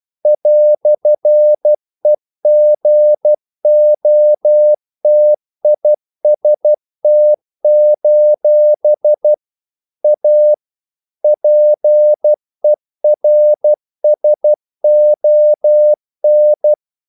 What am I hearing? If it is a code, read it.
Morse code, heard as a keyed tone, about 12 wpm, 600 Hz. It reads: "EGOTIST: A PERSON
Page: English quotations in CW (telegraphy - morse)